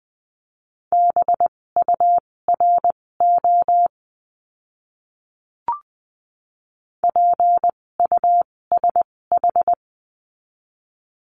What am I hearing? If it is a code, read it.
Morse code, heard as a keyed tone, about 20 wpm, spaced as for 16 wpm, 700 Hz. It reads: BURO PUSH
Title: 256 Lesson 42a - Words 12-20wpm